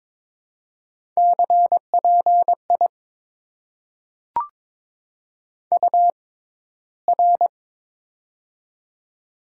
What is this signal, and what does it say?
Morse code, heard as a keyed tone, about 22 wpm, spaced as for 14 wpm, 700 Hz. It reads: CPI UR
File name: Top 90 QSO Elements - 6x Character Spacing - 22wpm